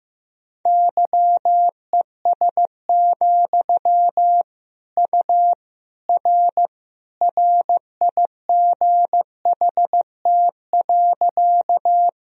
Morse code, heard as a keyed tone, about 16 wpm, 700 Hz. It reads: YES, U R RIGHT.
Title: Sentences from Top 500 Words and Abbreviations 15wpm